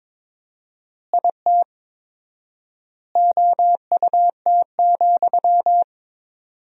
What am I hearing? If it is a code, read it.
Morse code, heard as a keyed tone, about 22 wpm, 700 Hz. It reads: IT OUT,
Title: 4x Word-Spacing - Sentences from Top 300 Words - 22wpm